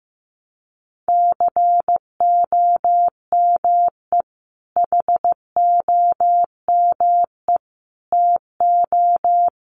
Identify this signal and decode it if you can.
Morse code, heard as a keyed tone, about 15 wpm, 700 Hz. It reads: COME HOME TO
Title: Sentences from Top 1000 Words 15wpm